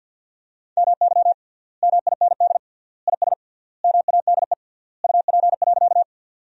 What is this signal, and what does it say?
Morse code, heard as a keyed tone, about 52 wpm, 700 Hz. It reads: MY MIND IS MADE UP.
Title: 3x Word-Spacing - Sentences from Top 500 Words - 50wpm